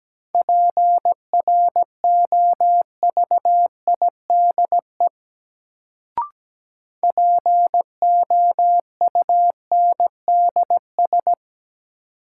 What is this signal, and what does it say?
Morse code, heard as a keyed tone, about 17 wpm, 700 Hz. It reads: PROVIDE POUNDS
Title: Sentences from Top 1000 Words - Review Part 2 - New Word Drill - 17wpm